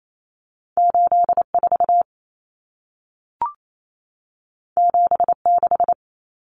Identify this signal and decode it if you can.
Morse code, heard as a keyed tone, about 28 wpm, 700 Hz. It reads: 84 76
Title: Numbers - 2-Digit 28wpm